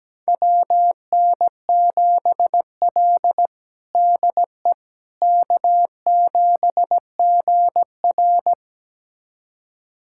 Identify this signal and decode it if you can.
Morse code, heard as a keyed tone, about 17 wpm, 700 Hz. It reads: WN7L DE K7GR
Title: USA Callsign DE Callsign - 4 Characters - 17wpm